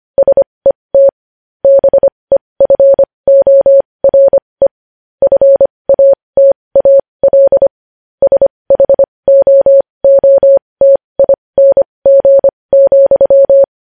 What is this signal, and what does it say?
Morse code, heard as a keyed tone, about 25 wpm, spaced as for 20 wpm, 550 Hz. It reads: SET BEFORE FATAL SHOOTING,
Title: News Headlines at 20